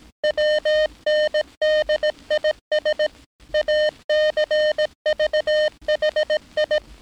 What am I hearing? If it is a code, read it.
Morse code, heard as a keyed tone, about 18 wpm, 610 Hz. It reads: WNDIS ACVHI
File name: training / NAVPERS 11049to11060 / NAVPERS 11059 RA RB